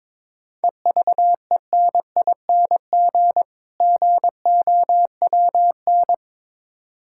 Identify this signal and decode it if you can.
Morse code, heard as a keyed tone, about 22 wpm, 690 Hz. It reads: EVENING GOWN
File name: Difficult Words with Hints - 22wpm